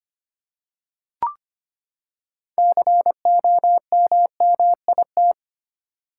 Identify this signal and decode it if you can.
Morse code, heard as a keyed tone, about 25 wpm, 700 Hz. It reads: COMMIT
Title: Six Letter Words - 25wpm